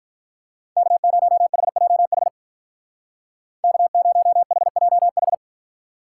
Morse code, heard as a keyed tone, about 43 wpm, 700 Hz. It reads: K0HJH K0HJH
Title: Speed Racing - USA 1x3 Call Signs 35wpm